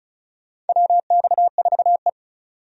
Morse code, heard as a keyed tone, about 34 wpm, 700 Hz. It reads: WX4E
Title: USA Call Signs - 4 Characters - 35wpm